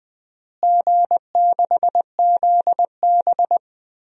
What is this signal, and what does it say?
Morse code, heard as a keyed tone, about 20 wpm, 700 Hz. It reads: G6ZB
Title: DXCC-100 Call Signs - 4 Characters - Repeated 3x - 20wpm